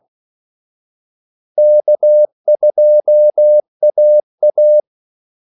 Morse code, heard as a keyed tone, about 16 wpm, 600 Hz. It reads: K2AA
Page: Kaldesignaler 16 wpm